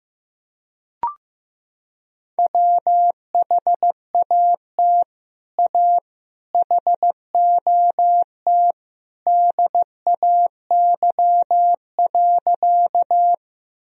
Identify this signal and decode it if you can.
Morse code, heard as a keyed tone, about 15 wpm, 700 Hz. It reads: WHAT A HOT DAY.
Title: Sentences from Top 1000 Words - Review Part 3 - 15wpm